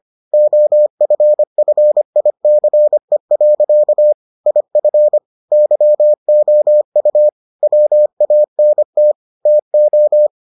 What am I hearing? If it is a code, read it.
Morse code, heard as a keyed tone, about 25 wpm, 600 Hz. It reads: OFFICE. IF YOU WANT TO